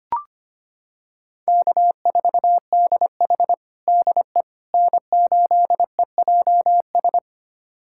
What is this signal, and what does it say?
Morse code, heard as a keyed tone, about 25 wpm, 700 Hz. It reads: K4DH DE N8EJS
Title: USA Callsign DE Callsign 25wpm